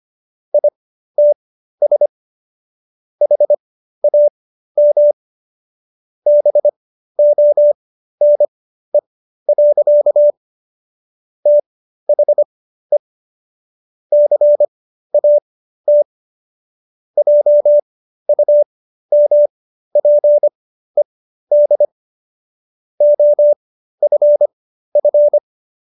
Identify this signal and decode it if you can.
Morse code, heard as a keyed tone, about 25 wpm, spaced as for 13 wpm, 590 Hz. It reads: ITS HAM BONE. THE CAT JUMPED OFF